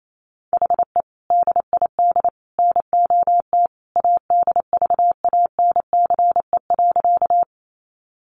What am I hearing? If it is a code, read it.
Morse code, heard as a keyed tone, about 28 wpm, 700 Hz. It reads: HE DID NOT ADVANCE.
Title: Sentences from Top 1400 Words 28wpm